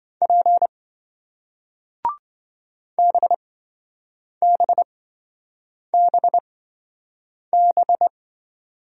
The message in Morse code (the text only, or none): P BBBB